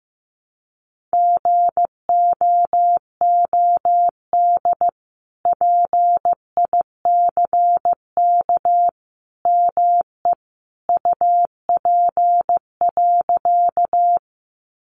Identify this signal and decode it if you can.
Morse code, heard as a keyed tone, about 15 wpm, 700 Hz. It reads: GOOD PICK ME UP.
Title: Sentences from Top 1000 Words 15wpm